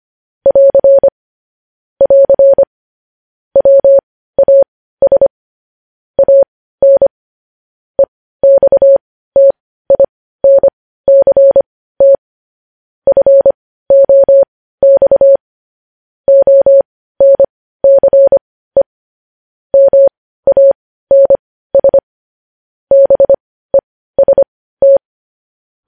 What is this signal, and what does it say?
Morse code, heard as a keyed tone, about 25 wpm, spaced as for 15 wpm, 550 Hz. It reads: <AR> <AR> WAS AN EXTINCT FOX ONCE MANS BEST
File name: podcast / News headlines-1x-25x15